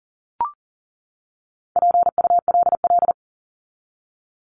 Morse code, heard as a keyed tone, about 40 wpm, 700 Hz. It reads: PULL
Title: Top 500 Words 40wpm